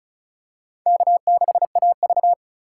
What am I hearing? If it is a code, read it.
Morse code, heard as a keyed tone, about 36 wpm, 700 Hz. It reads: K6AV
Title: USA Call Signs - 4 Characters - 4x Character Spacing - 35wpm